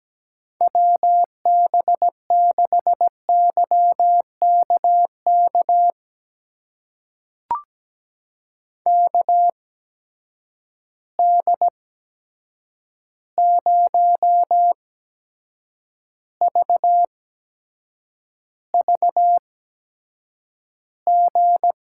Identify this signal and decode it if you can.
Morse code, heard as a keyed tone, about 17 wpm, spaced as for 7 wpm, 700 Hz. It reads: WB6YKK KD0VVG
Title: USA Call Signs - 6 Characters - 8x Character Spacing - 17wpm